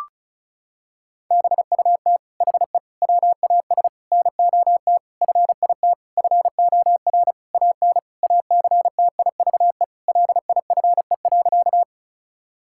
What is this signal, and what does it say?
Morse code, heard as a keyed tone, about 36 wpm, 700 Hz. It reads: BUT HE WAS NOT FIT FOR AN ACTIVE LIFE.
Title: Sentences from Top 1000 Words - Review Part 6 - 35wpm